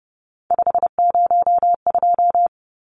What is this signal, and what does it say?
Morse code, heard as a keyed tone, about 30 wpm, 700 Hz. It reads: 502
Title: Numbers - 3-Digit 30wpm